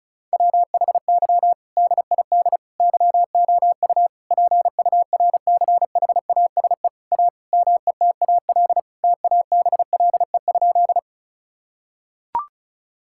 Morse code, heard as a keyed tone, about 36 wpm, 700 Hz. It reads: WHY DID YOU PURCHASE A METAL TABLE?
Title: Sentences from Top 1000 Words - Review Part 6 - 35wpm